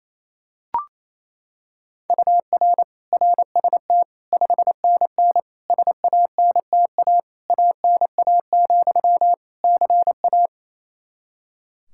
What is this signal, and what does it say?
Morse code, heard as a keyed tone, about 28 wpm, 700 Hz. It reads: UR RST 5NN SANTA ANA, CA
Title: Practice QSO - Exchange 1 - Beginning - 28wpm